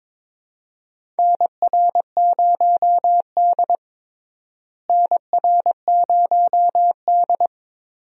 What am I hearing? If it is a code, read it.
Morse code, heard as a keyed tone, about 22 wpm, 700 Hz. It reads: NR0D NR0D
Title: USA Call Signs - 4 Characters - Repeated 3x - 22wpm